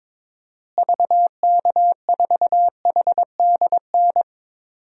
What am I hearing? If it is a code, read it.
Morse code, heard as a keyed tone, about 22 wpm, 700 Hz. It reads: VK4HDN
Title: DXCC-100 Call Signs - 6 Characters - Repeated 3x - 22wpm